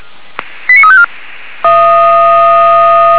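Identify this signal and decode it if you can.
Five-tone selective call (ZVEI), heard as quick runs of five tones, tones of about 70 ms. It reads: 89354
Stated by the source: ZVEI five-tone call received by SDR.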